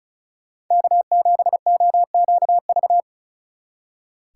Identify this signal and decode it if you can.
Morse code, heard as a keyed tone, about 44 wpm, 700 Hz. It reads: K7OQV
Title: Speed Racing - USA 1x3 Call Signs 35wpm